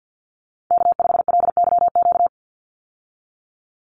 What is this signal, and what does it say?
Morse code, heard as a keyed tone, about 48 wpm, 700 Hz. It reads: K5LYQ
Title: USA Call Signs 50wpm